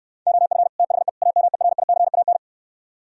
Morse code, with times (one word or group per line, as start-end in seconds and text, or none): 0.3-0.7 OF
0.8-1.1 THE
1.2-2.4 AMERICAN